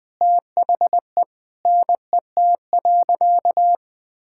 Morse code, heard as a keyed tone, about 19 wpm, 700 Hz. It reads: THE NET.
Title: Sentences with 9-Word Phrases - 20wpm